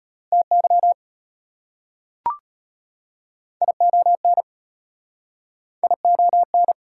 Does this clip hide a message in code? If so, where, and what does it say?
Yes: Morse code from 0.3 to 6.7 s, heard as a keyed tone, about 34 wpm, 690 Hz. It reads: TY ION ION